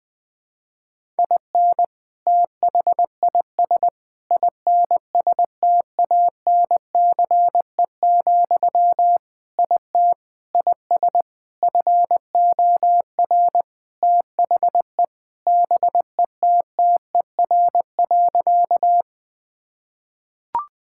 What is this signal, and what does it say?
Morse code, heard as a keyed tone, about 20 wpm, 700 Hz. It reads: IN THIS INSTANCE, IT IS FOR THE BETTER.
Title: Sentences from Top 1000 Words - Review Part 5 - 20wpm